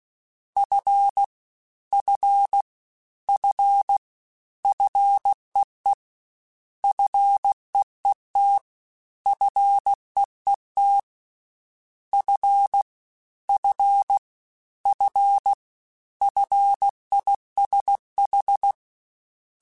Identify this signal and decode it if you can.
Morse code, heard as a keyed tone, about 16 wpm, 800 Hz. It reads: F F F FEE FEET FEET F F F FISH
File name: Prefixes 3x2 Twice @15